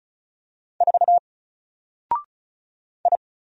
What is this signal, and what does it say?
Morse code, heard as a keyed tone, about 36 wpm, 700 Hz. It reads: 4 I
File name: Single-Letter-Number-35wpm